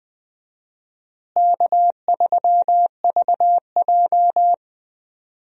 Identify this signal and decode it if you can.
Morse code, heard as a keyed tone, about 20 wpm, 700 Hz. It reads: K3VJ
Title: USA Call Signs - 4 Characters - Repeated 3x - 20wpm